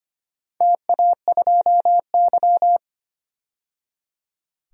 Morse code, heard as a keyed tone, about 25 wpm, 700 Hz. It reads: TA2Y
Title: DXCC-100 Call Signs - 4 Characters - 25wpm